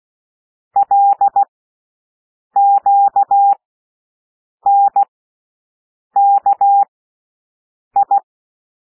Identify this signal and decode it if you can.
Morse code, heard as a keyed tone, about 16 wpm, spaced as for 6 wpm, 800 Hz. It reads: LQNKI
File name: cwmp3 / L1 SOUND LETTERS 16 wpm